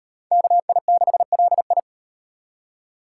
Morse code, heard as a keyed tone, about 35 wpm, 700 Hz. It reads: KI6LI